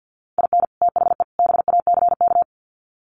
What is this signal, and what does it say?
Morse code, heard as a keyed tone, about 50 wpm, 700 Hz. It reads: IN THE BACK
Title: Sets of 3 Words 50wpm